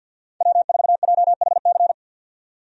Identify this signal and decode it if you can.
Morse code, heard as a keyed tone, about 52 wpm, 700 Hz. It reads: W4JHC
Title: USA Call Signs - 5 Characters - 50wpm